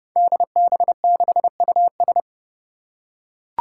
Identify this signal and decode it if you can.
Morse code, heard as a keyed tone, about 30 wpm, 700 Hz. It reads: DB6US
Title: Five Challenging Letter-Number Combinations 30wpm